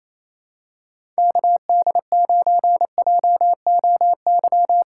Morse code, heard as a keyed tone, about 28 wpm, 700 Hz. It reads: KD9JOY
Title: USA Callsign DE Callsign - 6 Characters - 28wpm